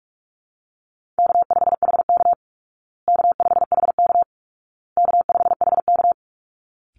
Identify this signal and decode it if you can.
Morse code, heard as a keyed tone, about 44 wpm, 700 Hz. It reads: K5HK K5HK K5HK
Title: Call Signs Repeated 2x - 45wpm